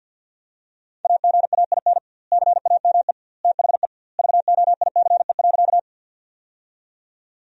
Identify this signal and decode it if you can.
Morse code, heard as a keyed tone, about 49 wpm, 700 Hz. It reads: AGAIN CAME THE VOICE.
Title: 2x Word-Spacing - Sentences from Top 500 Words - 50wpm